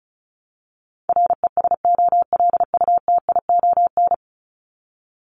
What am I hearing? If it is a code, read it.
Morse code, heard as a keyed tone, about 34 wpm, 700 Hz. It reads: RESOLUTION
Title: Top 1501-1750 Words 35wpm